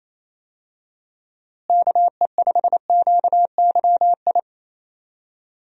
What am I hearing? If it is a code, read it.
Morse code, heard as a keyed tone, about 28 wpm, 700 Hz. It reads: KE5QYI
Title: USA Call Signs - 6 Characters - Repeated 2x - 28wpm